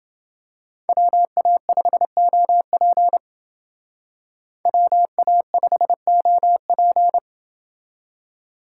Morse code, heard as a keyed tone, about 28 wpm, 700 Hz. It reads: WA5OP WA5OP